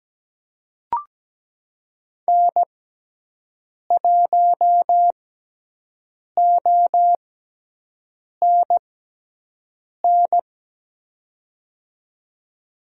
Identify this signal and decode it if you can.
Morse code, heard as a keyed tone, about 17 wpm, spaced as for 6 wpm, 700 Hz. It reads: N1ONN
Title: USA Call Signs - 6x Character Spacing - 17wpm